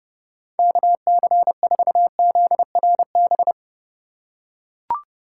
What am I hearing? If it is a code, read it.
Morse code, heard as a keyed tone, about 32 wpm, 700 Hz. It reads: KC4ZRB